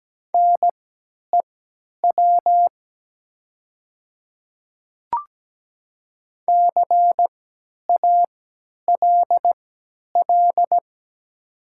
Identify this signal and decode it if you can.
Morse code, heard as a keyed tone, about 17 wpm, spaced as for 10 wpm, 700 Hz. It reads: NEW CALL
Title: 3x Character Spacing - Top 500 Words - 17wpm